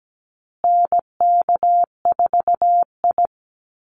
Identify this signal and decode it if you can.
Morse code, heard as a keyed tone, about 17 wpm, 700 Hz. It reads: NK4I
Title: Call Signs Repeated 3x - 17wpm